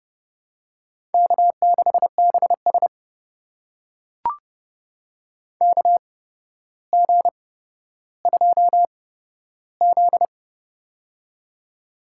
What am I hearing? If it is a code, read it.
Morse code, heard as a keyed tone, about 32 wpm, spaced as for 13 wpm, 700 Hz. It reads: K6BS KG2Z
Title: USA Call Signs - 4 Characters - 8x Character Spacing - 30wpm